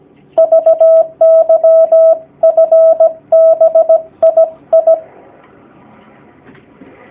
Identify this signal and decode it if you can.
Morse code, heard as a keyed tone, about 17 wpm, 650 Hz. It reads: VYFBII
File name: jh8cbh / 2019cw / sp16- w